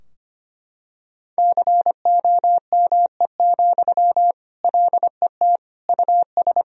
Morse code, heard as a keyed tone, about 25 wpm, 700 Hz. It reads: COME, LET US
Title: Sentences from Top 1000 Words - Review Part 4 - 25wpm